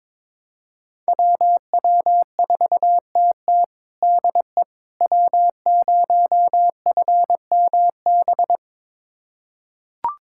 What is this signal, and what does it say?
Morse code, heard as a keyed tone, about 22 wpm, 700 Hz. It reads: WW4TT DE W0FMB
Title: USA Callsign DE Callsign - 5 Characters - 22wpm